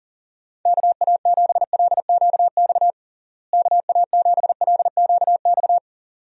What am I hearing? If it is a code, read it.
Morse code, heard as a keyed tone, about 37 wpm, 700 Hz. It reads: KA7LQX KA7LQX
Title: USA Call Signs - 6 Characters - Repeated 2x - 40wpm